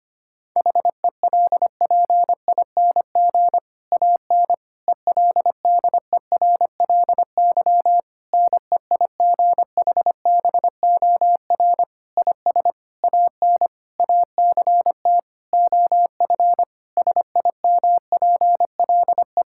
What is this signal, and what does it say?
Morse code, heard as a keyed tone, about 25 wpm, 700 Hz. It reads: HELPING AN ELDERLY NEIGHBOR IS AN ACT OF SIMPLE
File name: practice / 25wpm / Difficult Sentences with Hints - 25wpm